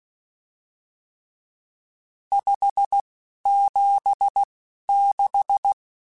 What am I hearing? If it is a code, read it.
Morse code, heard as a keyed tone, about 16 wpm, spaced as for 11 wpm, 800 Hz. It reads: 576
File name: RSTs / RST-mp3s / Various RST @10